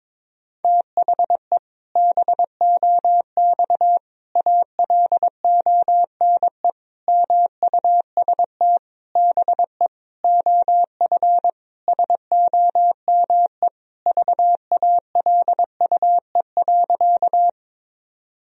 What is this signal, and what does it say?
Morse code, heard as a keyed tone, about 22 wpm, 700 Hz. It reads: THE BOX ALONE MUST BE OF SOME VALUE.
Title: Sentences from Top 1000 Words - Review Part 3 - 22wpm